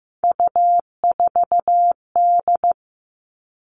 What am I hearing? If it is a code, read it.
Morse code, heard as a keyed tone, about 15 wpm, 700 Hz. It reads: U4D
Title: All Cut Numbers - 3-Digit 15wpm